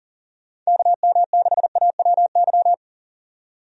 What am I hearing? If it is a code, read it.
Morse code, heard as a keyed tone, about 40 wpm, 700 Hz. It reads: KM6AWY
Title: USA Call Signs - 6 Characters - 40wpm